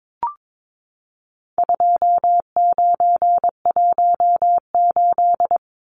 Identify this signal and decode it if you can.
Morse code, heard as a keyed tone, about 22 wpm, 700 Hz. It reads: 2918
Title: Numbers - 4-Digit 22wpm